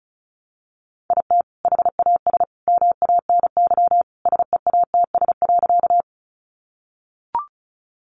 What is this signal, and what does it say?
Morse code, heard as a keyed tone, about 34 wpm, 700 Hz. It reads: IT HAS MANY SEATS.
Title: Additional Easy Phrases 35wpm